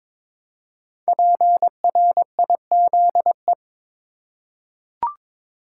Morse code, heard as a keyed tone, about 22 wpm, 700 Hz. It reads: PRIZE
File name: 5 Letter Words - First 3 Letters Given - 22wpm